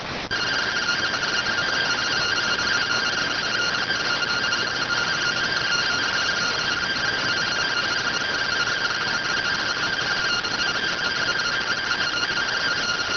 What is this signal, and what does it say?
Two-tone teleprinter signal, about 170 Hz shift.